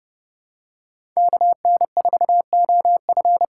Morse code, heard as a keyed tone, about 30 wpm, 700 Hz. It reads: KN4OF
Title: USA Call Signs - 5 Characters - 4x Character Spacing - 30wpm